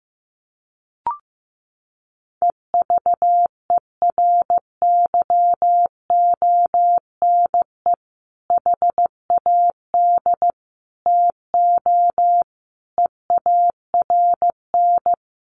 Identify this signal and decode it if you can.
Morse code, heard as a keyed tone, about 15 wpm, 700 Hz. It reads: EVERYONE HAD TO EARN
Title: Sentences from Top 1500 Words 15wpm